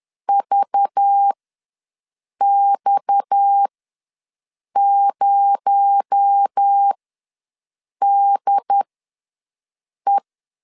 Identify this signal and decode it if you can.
Morse code, heard as a keyed tone, about 10 wpm, spaced as for 6 wpm, 800 Hz. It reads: VX0DE
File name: LEZIONI / 6 PKX0 / 6 PKX0 ws 10wpm 4220